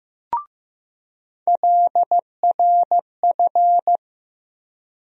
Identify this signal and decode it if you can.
Morse code, heard as a keyed tone, about 16 wpm, 700 Hz. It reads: LRF
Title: USA Call Sign Postfixes - 3 Letters 15wpm